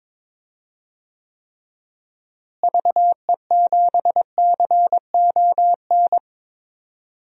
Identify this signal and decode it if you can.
Morse code, heard as a keyed tone, about 22 wpm, 700 Hz. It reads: VE7CON
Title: DXCC-100 Call Signs - 6 Characters - Repeated 2x - 22wpm